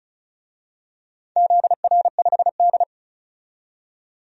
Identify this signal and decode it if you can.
Morse code, heard as a keyed tone, about 34 wpm, 700 Hz. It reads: ZR5D
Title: DXCC-100 Call Signs - 4 Characters - 35wpm